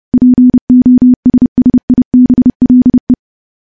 Morse code, heard as a keyed tone, about 28 wpm, 250 Hz. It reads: POSSIBLE
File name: morse / morse-training-30WPM